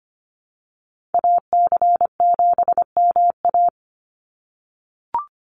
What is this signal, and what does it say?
Morse code, heard as a keyed tone, about 25 wpm, 700 Hz. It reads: AC7MA
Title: NA SOTA Activator Callsigns 25wpm